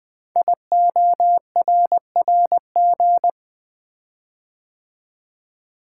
Morse code, heard as a keyed tone, about 20 wpm, 700 Hz. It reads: IORRG
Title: Code Groups - 5 Letters - 20wpm